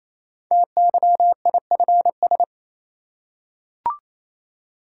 Morse code, heard as a keyed tone, about 28 wpm, 690 Hz. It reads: TYIFS
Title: Code Groups - 5 Letter:Numbers - 28wpm